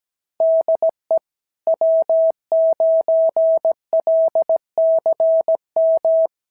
Morse code, heard as a keyed tone, about 17 wpm, 650 Hz. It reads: DE W9LCM
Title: SOTA QSO Practice 17wpm